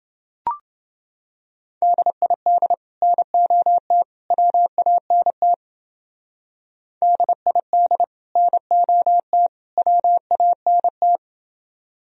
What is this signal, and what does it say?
Morse code, heard as a keyed tone, about 28 wpm, 700 Hz. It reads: DID NOT WANT DID NOT WANT